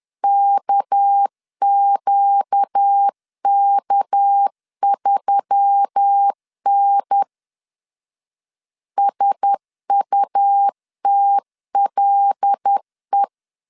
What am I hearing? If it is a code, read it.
Morse code, heard as a keyed tone, about 10 wpm, 800 Hz. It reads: KQK3N SUTLE
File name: LEZIONI / 6 PKX0 / 6 PK 10wpm 2436